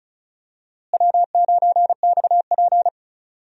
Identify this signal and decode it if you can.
Morse code, heard as a keyed tone, about 34 wpm, 700 Hz. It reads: W9XP
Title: USA Call Signs - 4 Characters - Repeated 3x - 35wpm